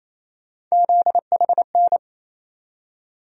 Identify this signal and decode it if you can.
Morse code, heard as a keyed tone, about 28 wpm, 700 Hz. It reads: ZHN
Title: USA Call Sign Postfixes - 3 Letters 28wpm